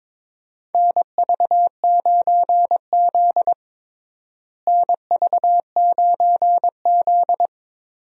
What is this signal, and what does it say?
Morse code, heard as a keyed tone, about 22 wpm, 700 Hz. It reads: NV9Z NV9Z
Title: USA Call Signs - 4 Characters - Repeated 2x - 22wpm